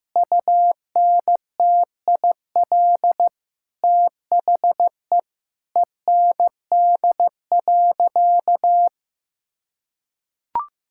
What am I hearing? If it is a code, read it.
Morse code, heard as a keyed tone, about 15 wpm, 700 Hz. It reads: UNTIL THE END.
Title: Sentences from Top 1000 Words - Review Part 5 - 15wpm